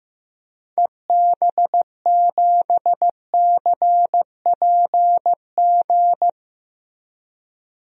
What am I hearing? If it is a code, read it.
Morse code, heard as a keyed tone, about 16 wpm, 700 Hz. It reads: EB7CPG
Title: DXCC-100 Call Signs - 6 Characters - Repeated 2x - 15wpm